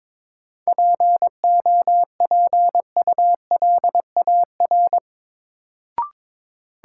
Morse code, heard as a keyed tone, about 22 wpm, 700 Hz. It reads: POPULAR
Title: Sentences from Top 1000 Words - Review Part 3 - New Word Drill - 22wpm